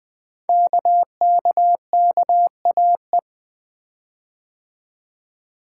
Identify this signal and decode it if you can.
Morse code, heard as a keyed tone, about 20 wpm, 700 Hz. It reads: KKKAE
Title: Code Groups - 5 Letters - 20wpm